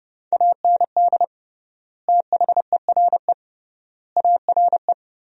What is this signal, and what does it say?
Morse code, heard as a keyed tone, about 30 wpm, 700 Hz. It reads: AND THERE ARE
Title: 3x Word-Spacing - Sentences from Top 300 Words - 30wpm